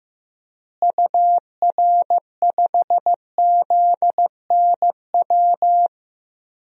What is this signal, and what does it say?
Morse code, heard as a keyed tone, about 15 wpm, 700 Hz. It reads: UR5ZNW
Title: DXCC-100 Call Signs - 6 Characters - Repeated 3x - 15wpm